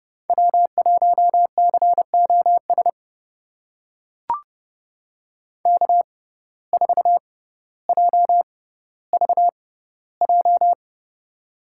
Morse code, heard as a keyed tone, about 30 wpm, spaced as for 15 wpm, 700 Hz. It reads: W1COS K4JVJ
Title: USA Call Signs - 5 Characters - 6x Character Spacing - 30wpm